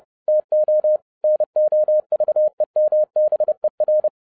Morse code, heard as a keyed tone, about 30 wpm, 610 Hz. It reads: TO NOVEMBER
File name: 2012-01-mp3 / n11-I DIDNT MISS THE SOLAR CYCLE 24-30wpm